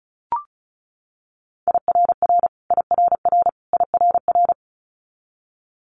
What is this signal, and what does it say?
Morse code, heard as a keyed tone, about 36 wpm, 700 Hz. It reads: IRR IRR IRR
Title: Multi-letter Phonemes Repeated 3x 35wpm